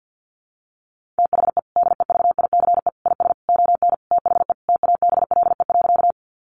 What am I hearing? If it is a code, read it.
Morse code, heard as a keyed tone, about 50 wpm, 700 Hz. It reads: THE DEVICE IS ON THE TABLE.
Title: Sentences from Top 1500 Words 50wpm